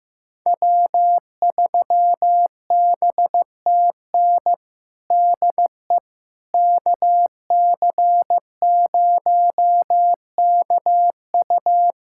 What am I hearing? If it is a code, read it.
Morse code, heard as a keyed tone, about 15 wpm, 700 Hz. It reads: W3BTN DE KC0KU
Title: USA Callsign DE Callsign - 5 Characters - 15wpm